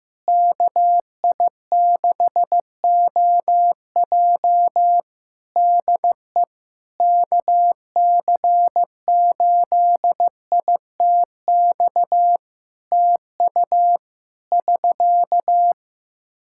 Morse code, heard as a keyed tone, about 15 wpm, 700 Hz. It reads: KI6OJ DE KC8ITX TU <SK>
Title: K1USN SST QSO Practice 15wpm